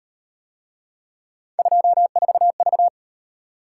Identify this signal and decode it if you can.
Morse code, heard as a keyed tone, about 36 wpm, 700 Hz. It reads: 24V